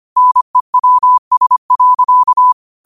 Morse code, heard as a keyed tone, about 25 wpm, 1000 Hz. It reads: NEWS.
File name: qotdfiles / 25wpm